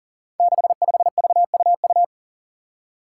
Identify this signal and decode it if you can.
Morse code, heard as a keyed tone, about 40 wpm, 700 Hz. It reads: 65VUU
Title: Five Challenging Letter-Number Combinations 40wpm